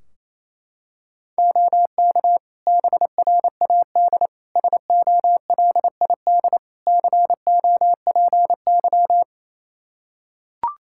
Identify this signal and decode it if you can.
Morse code, heard as a keyed tone, about 27 wpm, 700 Hz. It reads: OK BRAD SOLID COPY
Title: K1USN SST QSO Practice 28wpm